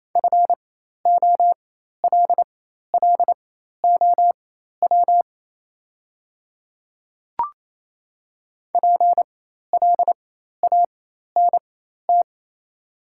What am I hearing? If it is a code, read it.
Morse code, heard as a keyed tone, about 28 wpm, spaced as for 13 wpm, 690 Hz. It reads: FOLLOW PLANT
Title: 4x Character Spacing - Top 300 Words - 28wpm